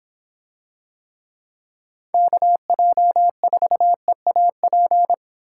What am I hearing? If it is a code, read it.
Morse code, heard as a keyed tone, about 26 wpm, 700 Hz. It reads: KJ4EAP